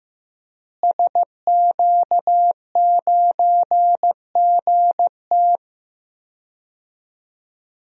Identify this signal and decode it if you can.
Morse code, heard as a keyed tone, about 15 wpm, 700 Hz. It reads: SQ9GT
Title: DXCC-100 Call Signs - 5 Characters - Repeated 3x - 15wpm